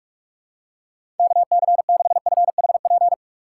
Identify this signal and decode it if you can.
Morse code, heard as a keyed tone, about 48 wpm, 700 Hz. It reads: KC6FHP